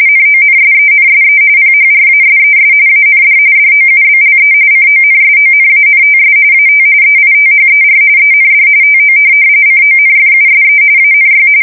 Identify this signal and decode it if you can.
RTTY, about 45.45 baud. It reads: NONSENSE PHRASES AND DOUBLE CONSONANTS AT EITHER END TO CONFUSE ANY
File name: fsk170 / USS-New-Jersey